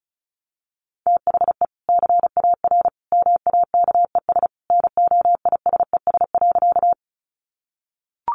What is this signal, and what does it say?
Morse code, heard as a keyed tone, about 34 wpm, 700 Hz. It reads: THE CAR MAKES NOISES.
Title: Easy Phrases 35wpm